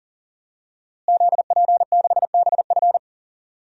Morse code, heard as a keyed tone, about 40 wpm, 700 Hz. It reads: ZP6BF
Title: DXCC-100 Call Signs - 5 Characters - Repeated 3x - 40wpm